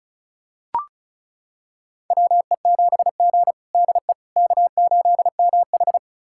Morse code, heard as a keyed tone, about 36 wpm, 700 Hz. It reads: WE7G DE K8MH
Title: USA Callsign DE Callsign - 4 Characters - 35wpm